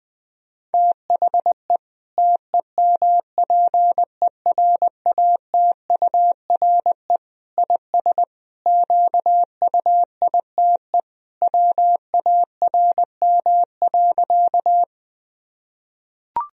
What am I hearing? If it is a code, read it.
Morse code, heard as a keyed tone, about 20 wpm, 700 Hz. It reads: THE TEMPERATURE IS QUITE WARM.
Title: Sentences from Top 1000 Words - Review Part 4 - 20wpm